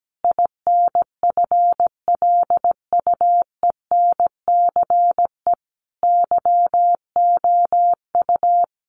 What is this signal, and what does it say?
Morse code, heard as a keyed tone, about 17 wpm, 700 Hz. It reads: INFLUENCE YOU
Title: Sentences from Top 1100 Words 17wpm